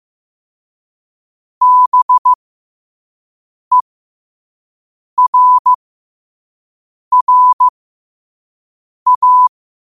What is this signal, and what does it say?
Morse code, heard as a keyed tone, about 15 wpm, spaced as for 5 wpm, 1000 Hz. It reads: BERRA
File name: qotdfiles / 5wpm